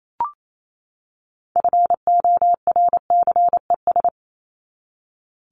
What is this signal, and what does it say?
Morse code, heard as a keyed tone, about 28 wpm, 700 Hz. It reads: FORCES
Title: ES:S Suffix - Top 100 Words with Base Words 28wpm